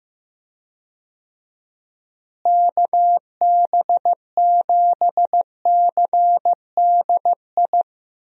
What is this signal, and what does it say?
Morse code, heard as a keyed tone, about 15 wpm, 700 Hz. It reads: KB7CDI
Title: USA Call Signs - 6 Characters - Repeated 3x - 15wpm